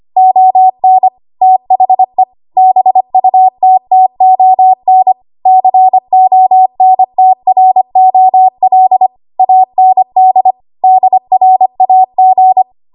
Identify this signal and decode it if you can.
Morse code, heard as a keyed tone, about 25 wpm, 750 Hz. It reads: ON THE BUTTON CONTROL AND DRAG
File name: Archive / 25 WPM / 25WPM